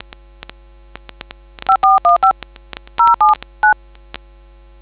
DTMF, heard as telephone touch tones, tones of about 140 ms. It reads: 5415 079